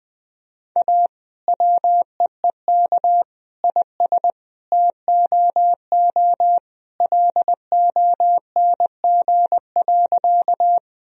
Morse code, heard as a keyed tone, about 20 wpm, 700 Hz. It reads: A WEEK IS TOO LONG.
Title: Sentences from Top 1000 Words - Review Part 2 - 20wpm